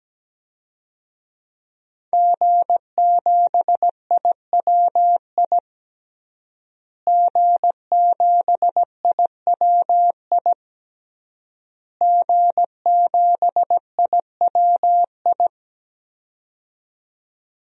Morse code, heard as a keyed tone, about 17 wpm, 700 Hz. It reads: G7IWI G7IWI G7IWI
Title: DXCC-100 Call Signs - 5 Characters - Repeated 3x - 17wpm